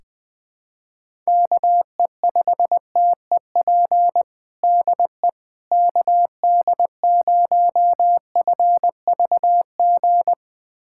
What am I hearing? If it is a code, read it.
Morse code, heard as a keyed tone, about 21 wpm, 700 Hz. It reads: KE5TEP DE KD0FVG
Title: USA Callsign DE Callsign - 6 Characters - 20wpm